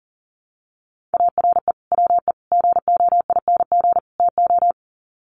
Morse code, heard as a keyed tone, about 40 wpm, 700 Hz. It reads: ARE WE GOING TO
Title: Sets of 4 Words 40wpm